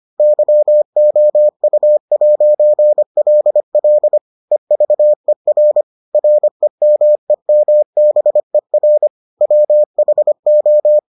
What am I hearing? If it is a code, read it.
Morse code, heard as a keyed tone, about 25 wpm, 600 Hz. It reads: YOU'LL EVER REMEMBER WHO